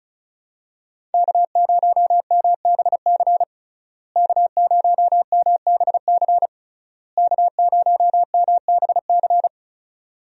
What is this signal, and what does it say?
Morse code, heard as a keyed tone, about 34 wpm, 700 Hz. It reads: K0MBC K0MBC K0MBC
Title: USA Call Signs - 5 Characters - Repeated 3x - 35wpm